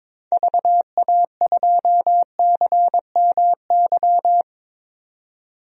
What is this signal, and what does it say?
Morse code, heard as a keyed tone, about 22 wpm, 700 Hz. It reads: VA2CMY
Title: DXCC-100 Call Signs - 6 Characters - 22wpm